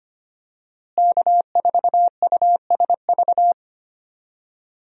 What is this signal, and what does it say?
Morse code, heard as a keyed tone, about 25 wpm, 700 Hz. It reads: K4USV
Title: USA Call Signs - 5 Characters - Repeated 2x - 25wpm